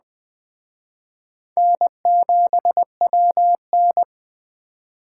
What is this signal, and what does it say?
Morse code, heard as a keyed tone, about 21 wpm, 700 Hz. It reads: N7WN
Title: Speed Racing - USA 1x2 Call Signs 17wpm